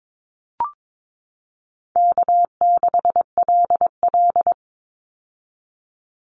Morse code, heard as a keyed tone, about 22 wpm, 700 Hz. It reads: K6LL
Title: Easy Call Signs - 22wpm